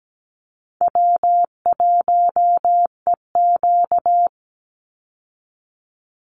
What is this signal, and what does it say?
Morse code, heard as a keyed tone, about 17 wpm, 700 Hz. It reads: W1EQ
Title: Easy Call Signs Repeated 2x - 17wpm